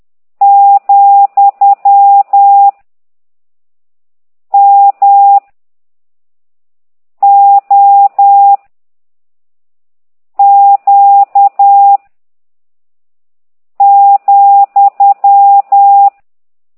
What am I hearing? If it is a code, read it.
Morse code, heard as a keyed tone, about 10 wpm, 800 Hz. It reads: , M O Q ,
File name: ct2idw / exerc13